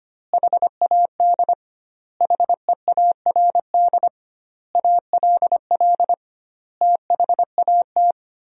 Morse code, heard as a keyed tone, about 25 wpm, 700 Hz. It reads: HAD HEARD ALL THAT
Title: 2x Word-Spacing - Sentences from Top 400 Words - 25wpm